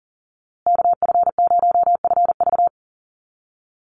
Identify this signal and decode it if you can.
Morse code, heard as a keyed tone, about 39 wpm, 700 Hz. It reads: KF0FV
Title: USA Call Signs Repeated 3x 40wpm